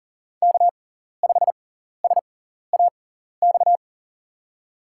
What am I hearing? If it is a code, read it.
Morse code, heard as a keyed tone, about 40 wpm, spaced as for 14 wpm, 700 Hz. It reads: K5SAX
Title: USA Call Signs - 5 Characters - 6x Character Spacing - 40wpm